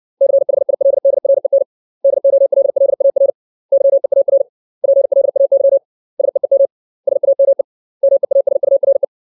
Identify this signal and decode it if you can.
Morse code, heard as a keyed tone, about 60 wpm, 550 Hz. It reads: CHILDREN DOLLAR YEAR PLAY SEEM SAME MEASURE
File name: 632 words / 632 Words 60WPM